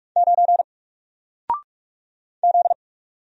Morse code, heard as a keyed tone, about 44 wpm, 700 Hz. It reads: 9 Z
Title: Single Letter-Number 45wpm